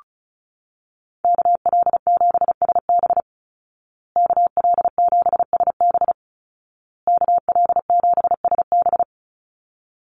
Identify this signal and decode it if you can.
Morse code, heard as a keyed tone, about 34 wpm, 700 Hz. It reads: KL7SB KL7SB KL7SB
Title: Call Signs Repeated 2x - 35wpm